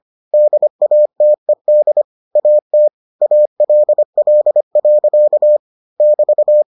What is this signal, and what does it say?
Morse code, heard as a keyed tone, about 25 wpm, 600 Hz. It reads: DATED AT ALL. <BT>